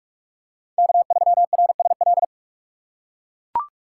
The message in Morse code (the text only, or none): K3RSL